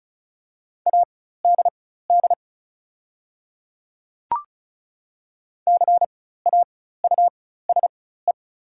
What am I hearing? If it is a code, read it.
Morse code, heard as a keyed tone, about 34 wpm, spaced as for 16 wpm, 700 Hz. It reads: ADD CAUSE
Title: 4x Character Spacing - Top 300 Words - 35wpm